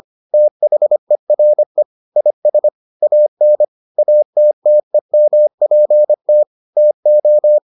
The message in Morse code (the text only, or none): THERE IS AN ATTEMPT TO